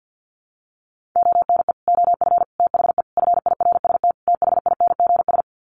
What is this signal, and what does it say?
Morse code, heard as a keyed tone, about 50 wpm, 700 Hz. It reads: ONE OF THE FIRST THINGS
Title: Sets of 5 Words Encore 50wpm